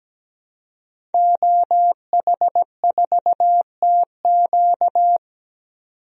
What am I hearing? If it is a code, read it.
Morse code, heard as a keyed tone, about 17 wpm, 700 Hz. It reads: OH4TQ
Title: DXCC-100 Call Signs - 5 Characters - 17wpm